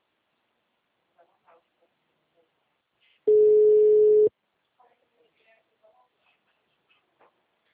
Ringback tone (Europe, 425 Hz).
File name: out- -2041- - - .68903